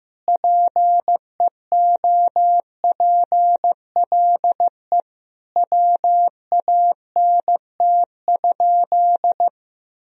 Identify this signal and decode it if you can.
Morse code, heard as a keyed tone, about 15 wpm, 700 Hz. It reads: PEOPLE WANT?
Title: Sentences from Top 1000 Words - Review Part 1 - 15wpm